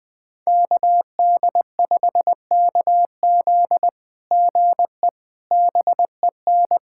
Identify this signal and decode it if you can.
Morse code, heard as a keyed tone, about 20 wpm, 700 Hz. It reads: KD5KZ GE BEN
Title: SOTA QSO Practice 20wpm